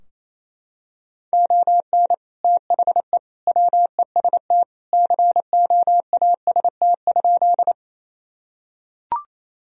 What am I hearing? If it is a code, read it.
Morse code, heard as a keyed tone, about 28 wpm, 700 Hz. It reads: ON THE WEST COAST?
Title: Sentences from Top 1000 Words - Review Part 4 - 28wpm